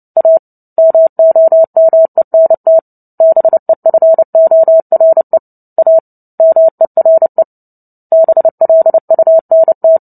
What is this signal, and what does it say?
Morse code, heard as a keyed tone, about 30 wpm, 650 Hz. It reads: A MOMENT BEFORE A MERE BLUNT
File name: Book Two / 03-Chapter 3 - 29 WPM